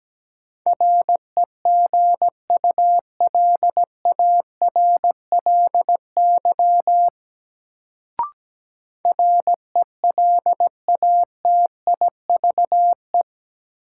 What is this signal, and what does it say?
Morse code, heard as a keyed tone, about 17 wpm, 700 Hz. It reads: REGULARLY RELATIVE
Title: Sentences from Top 1400 Words - New Word Drill - 17wpm